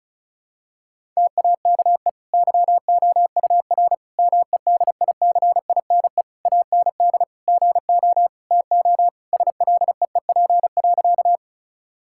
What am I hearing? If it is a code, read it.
Morse code, heard as a keyed tone, about 34 wpm, 700 Hz. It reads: TAKE YOUR MEDICINE AND GO TO SLEEP.
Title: Sentences from Top 1000 Words - Review Part 6 - 35wpm